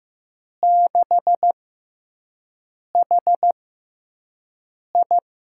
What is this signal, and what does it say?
Morse code, heard as a keyed tone, about 15 wpm, spaced as for 5 wpm, 700 Hz. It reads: 6HI
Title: USA Call Signs - 6 Characters - 6x Character Spacing - 15wpm